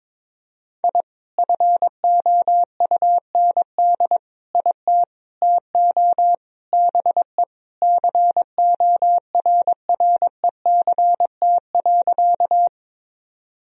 Morse code, heard as a keyed tone, about 21 wpm, 700 Hz. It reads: I FOUND IT TO BE CORRECT.
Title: Sentences from Top 1000 Words - Review Part 2 - 22wpm